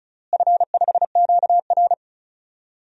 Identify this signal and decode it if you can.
Morse code, heard as a keyed tone, about 36 wpm, 700 Hz. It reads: F5QR
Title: DXCC-100 Call Signs - 4 Characters - Repeated 2x - 35wpm